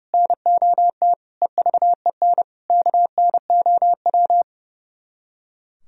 Morse code, heard as a keyed tone, about 37 wpm, 700 Hz. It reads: NOT EVEN KNOW
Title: Speed-Racing - Sets of 5 Words 30wpm